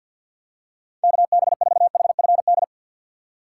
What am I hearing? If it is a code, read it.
Morse code, heard as a keyed tone, about 48 wpm, 700 Hz. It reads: KB4HFD
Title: USA Call Signs - 6 Characters - 3x Character Spacing - 50wpm